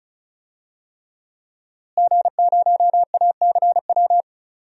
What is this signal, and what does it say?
Morse code, heard as a keyed tone, about 36 wpm, 700 Hz. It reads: G0ACW
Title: DXCC-100 Call Signs - 5 Characters - Repeated 3x - 35wpm